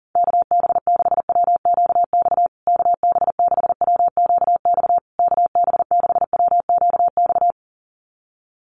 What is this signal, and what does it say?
Morse code, heard as a keyed tone, about 40 wpm, 700 Hz. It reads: KB6WQX KB6WQX KB6WQX
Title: USA Call Signs Repeated 3x 40wpm